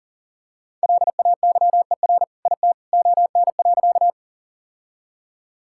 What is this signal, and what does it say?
Morse code, heard as a keyed tone, about 40 wpm, 700 Hz. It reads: LAYER IT ON.
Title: Sentences from Top 1600 Words 40wpm